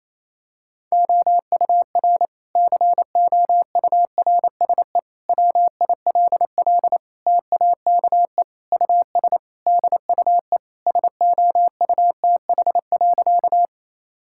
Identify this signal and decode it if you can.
Morse code, heard as a keyed tone, about 28 wpm, 700 Hz. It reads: OUR COURSE WILL TAKE US DUE SOUTH.
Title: Sentences from Top 1000 Words - Review Part 6 - 28wpm